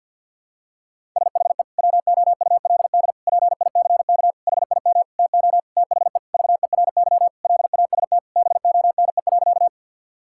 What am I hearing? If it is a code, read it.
Morse code, heard as a keyed tone, about 50 wpm, 700 Hz. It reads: SHE WOULD PICK HIM TO THE VERY LAST BONE.
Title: Sentences from Top 1600 Words 50wpm